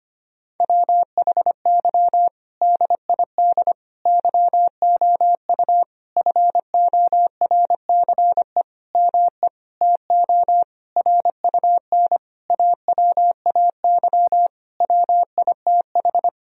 Morse code, heard as a keyed tone, about 25 wpm, 700 Hz. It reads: WHY DID YOU FORCE ME TO RUN AWAY WITH
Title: Sentences from Top 1000 Words - Review Part 3 - 25wpm